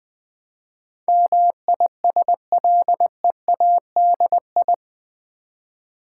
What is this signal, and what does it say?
Morse code, heard as a keyed tone, about 20 wpm, 700 Hz. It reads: MISLEADI
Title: Build A Word - 4 Letters At A Time - 10-Letter Words - 20wpm